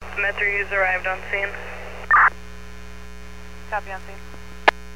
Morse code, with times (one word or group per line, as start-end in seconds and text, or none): none